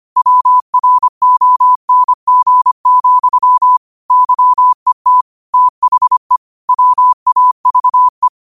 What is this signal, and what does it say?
Morse code, heard as a keyed tone, about 25 wpm, 1000 Hz. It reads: WRONG, YET THE WAVE
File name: qotdfiles / 25wpm